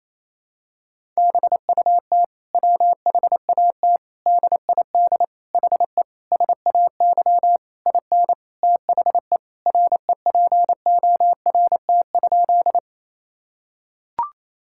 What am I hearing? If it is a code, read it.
Morse code, heard as a keyed tone, about 28 wpm, 700 Hz. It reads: BUT WHAT DID HE SAY IN THE REPORT?
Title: Sentences from Top 1000 Words - Review Part 6 - 28wpm